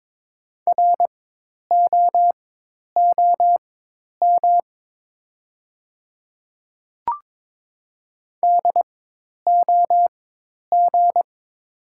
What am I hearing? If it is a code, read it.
Morse code, heard as a keyed tone, about 22 wpm, spaced as for 10 wpm, 700 Hz. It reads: ROOM DOG
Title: 4x Character Spacing - Top 500 Words - 22wpm